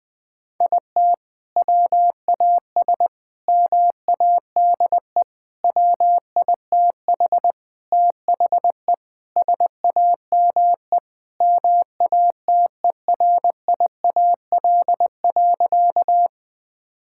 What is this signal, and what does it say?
Morse code, heard as a keyed tone, about 20 wpm, 700 Hz. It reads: IT WAS MADE WITH THE SAME MATERIAL.
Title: Sentences from Top 1000 Words - Review Part 5 - 20wpm